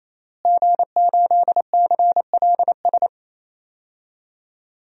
Morse code, heard as a keyed tone, about 28 wpm, 700 Hz. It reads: G8CLS
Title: DXCC-100 Call Signs - 5 Characters - Repeated 3x - 28wpm